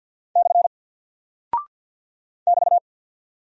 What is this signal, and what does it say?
Morse code, heard as a keyed tone, about 50 wpm, 700 Hz. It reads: / <BT>
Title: Single Character (Letter:Number:Punctuation:Prosign) 50wpm